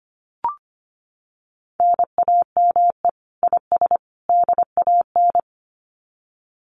Morse code, heard as a keyed tone, about 25 wpm, 700 Hz. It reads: NAME IS DAN
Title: Common QSO Phrases 25wpm